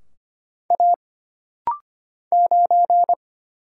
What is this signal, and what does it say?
Morse code, heard as a keyed tone, about 25 wpm, 700 Hz. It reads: A 9
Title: Single Letter-Number Alt - Rapid-Fire 25wpm